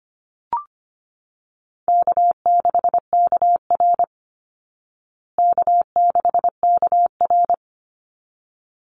Morse code, heard as a keyed tone, about 25 wpm, 700 Hz. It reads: K6KR K6KR
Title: Easy Call Signs Repeated 2x - 25wpm